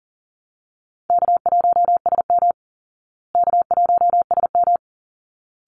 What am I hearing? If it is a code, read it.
Morse code, heard as a keyed tone, about 40 wpm, 700 Hz. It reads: K1SM K1SM
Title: Call Signs Repeated 3x - 40wpm